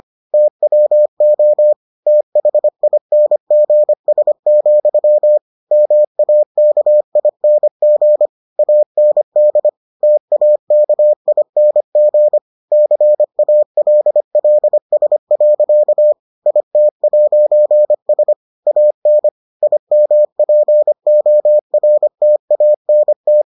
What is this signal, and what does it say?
Morse code, heard as a keyed tone, about 25 wpm, 600 Hz. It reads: TWO THINGS, MAKING AND TAKING CALLS. IT'S AN IMPORTANT